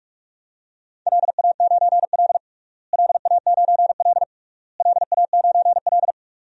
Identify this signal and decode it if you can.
Morse code, heard as a keyed tone, about 47 wpm, 700 Hz. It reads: LA9L LA9L LA9L
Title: DXCC-100 Call Signs - 4 Characters - Repeated 3x - 45wpm